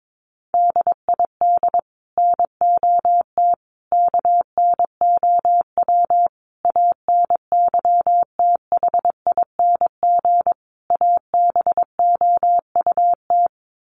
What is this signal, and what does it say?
Morse code, heard as a keyed tone, about 22 wpm, 700 Hz. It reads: DID NOT KNOW ANYTHING ABOUT
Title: Sets of 5 Words Encore2 22wpm